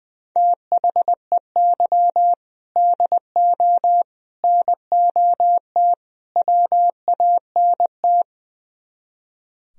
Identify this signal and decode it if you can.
Morse code, heard as a keyed tone, about 22 wpm, 700 Hz. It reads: THEY DO NOT WANT
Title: Speed-Racing - Sets of 4 Words 20wpm